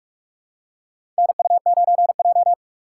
Morse code, heard as a keyed tone, about 46 wpm, 700 Hz. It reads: NU9J
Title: USA Call Signs - 4 Characters - 4x Character Spacing - 45wpm